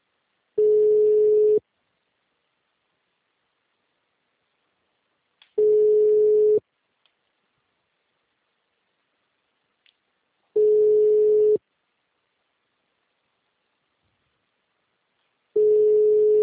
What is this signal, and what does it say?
Ringback tone (Europe, 425 Hz).